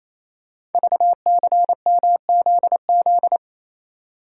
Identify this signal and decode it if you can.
Morse code, heard as a keyed tone, about 29 wpm, 700 Hz. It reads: VCMZZ
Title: Code Groups - 5 Letters - 28wpm